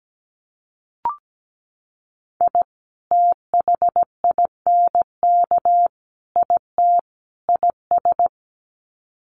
Morse code, heard as a keyed tone, about 17 wpm, 700 Hz. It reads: I THINK IT IS
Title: Sets of 4 Words 17wpm